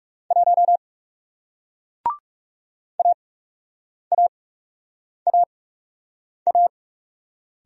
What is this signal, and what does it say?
Morse code, heard as a keyed tone, about 40 wpm, spaced as for 8 wpm, 700 Hz. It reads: 1 AAAA